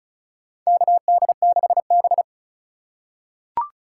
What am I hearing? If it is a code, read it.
Morse code, heard as a keyed tone, about 36 wpm, 700 Hz. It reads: KD6B
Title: USA Call Signs - 4 Characters - 3x Character Spacing - 35wpm